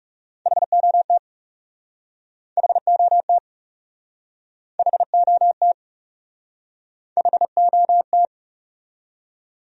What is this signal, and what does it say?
Morse code, heard as a keyed tone, about 40 wpm, 700 Hz. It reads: HOT HOT HOT HOT